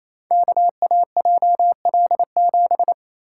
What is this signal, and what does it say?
Morse code, heard as a keyed tone, about 27 wpm, 700 Hz. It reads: KAJL7
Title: Code Groups - 5 Letter:Numbers - 28wpm